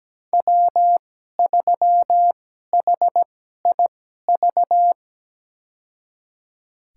Morse code, heard as a keyed tone, about 17 wpm, spaced as for 12 wpm, 700 Hz. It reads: W3HIV
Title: USA Call Signs - 5 Characters - 2x Character Spacing - 17wpm